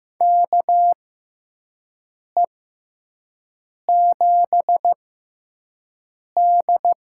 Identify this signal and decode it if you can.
Morse code, heard as a keyed tone, about 15 wpm, spaced as for 5 wpm, 700 Hz. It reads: KE7D
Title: USA Call Signs - 6x Character Spacing - 15wpm